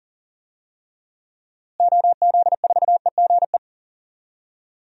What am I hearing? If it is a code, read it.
Morse code, heard as a keyed tone, about 40 wpm, 700 Hz. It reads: OZ4EGE
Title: DXCC-100 Call Signs - 6 Characters - Repeated 3x - 40wpm